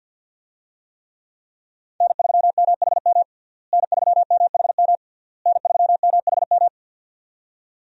Morse code, heard as a keyed tone, about 52 wpm, 690 Hz. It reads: N3MHM N3MHM N3MHM
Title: USA Call Signs - 5 Characters - Repeated 3x - 50wpm